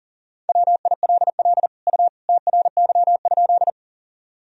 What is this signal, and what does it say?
Morse code, heard as a keyed tone, about 40 wpm, 700 Hz. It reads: WILL U TRY?
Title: Sentences from Top 200 Words and Abbreviations 40wpm